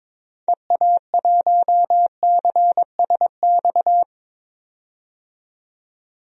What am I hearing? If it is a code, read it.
Morse code, heard as a keyed tone, about 22 wpm, 700 Hz. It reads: EA1CSX
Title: DXCC-100 Call Signs - 6 Characters - 22wpm